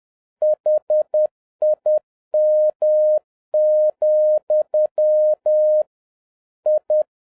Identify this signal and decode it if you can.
Morse code, heard as a keyed tone, about 10 wpm, 610 Hz. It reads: HIM, I